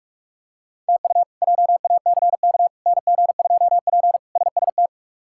Morse code, heard as a keyed tone, about 46 wpm, 700 Hz. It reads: TU JACK NG2P SST
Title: K1USN SST QSO Practice - 45wpm